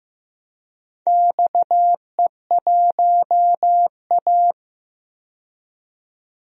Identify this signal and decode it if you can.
Morse code, heard as a keyed tone, about 15 wpm, 700 Hz. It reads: XE1A
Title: DXCC-100 Call Signs - 4 Characters - 15wpm